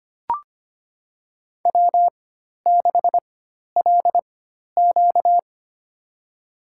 Morse code, heard as a keyed tone, about 25 wpm, spaced as for 12 wpm, 700 Hz. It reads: W6LQ
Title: USA Call Signs - 4 Characters - 4x Character Spacing - 25wpm